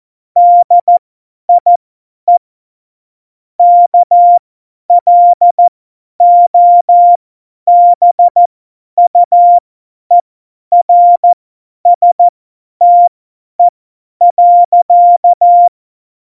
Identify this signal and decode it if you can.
Morse code, heard as a keyed tone, about 14 wpm, spaced as for 10 wpm, 700 Hz. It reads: DIE KLOBUERSTE.
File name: Morsekurs / Witze / Witz 28. November 2017, 50 F70